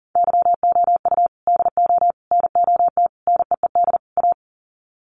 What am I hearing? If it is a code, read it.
Morse code, heard as a keyed tone, about 40 wpm, 700 Hz. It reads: YOU DO NOT NEED A
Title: Sets of 5 Words Encore 40wpm